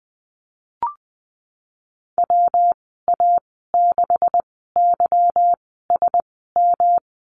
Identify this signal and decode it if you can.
Morse code, heard as a keyed tone, about 20 wpm, spaced as for 14 wpm, 700 Hz. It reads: WA6YSM
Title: 167 Lesson 27g - Number 7 - Call Signs 20wpm